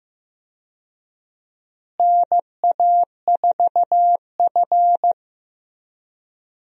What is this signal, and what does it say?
Morse code, heard as a keyed tone, about 15 wpm, 700 Hz. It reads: NA4F
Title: USA Call Signs - 4 Characters - Repeated 2x - 15wpm